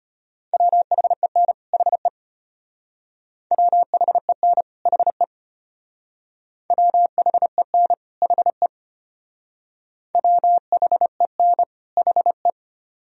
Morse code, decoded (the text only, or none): WHEN HE WHEN HE WHEN HE WHEN HE